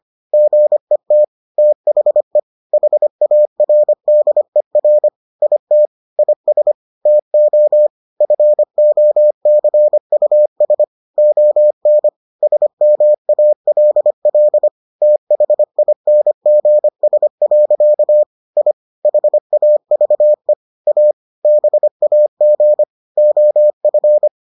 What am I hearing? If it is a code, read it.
Morse code, heard as a keyed tone, about 25 wpm, 600 Hz. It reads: GET THE HARDER IT IS TO FOCUS ON SMALL THINGS. I HAVE A BAG OF